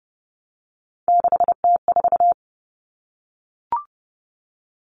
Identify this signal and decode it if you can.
Morse code, heard as a keyed tone, about 30 wpm, 700 Hz. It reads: 6T4
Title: Common Cut Numbers - 3-Digit 30wpm